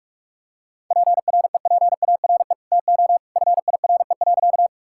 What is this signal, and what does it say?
Morse code, heard as a keyed tone, about 46 wpm, 700 Hz. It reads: PREPARE TO FIRE.
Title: Sentences from Top 1000 Words - Review Part 6 - 45wpm